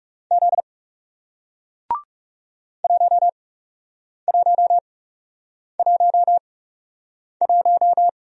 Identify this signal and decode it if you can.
Morse code, heard as a keyed tone, about 41 wpm, spaced as for 8 wpm, 700 Hz. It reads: Z 1111